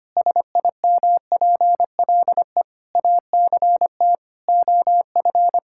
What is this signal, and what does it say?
Morse code, heard as a keyed tone, about 25 wpm, 700 Hz. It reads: SIMPLE ACT OF
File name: practice / 25wpm / Difficult Sentences with Hints - 25wpm